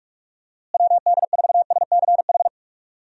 Morse code, heard as a keyed tone, about 42 wpm, 700 Hz. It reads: WD4SCH
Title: USA Call Signs - 6 Characters - Repeated 2x - 45wpm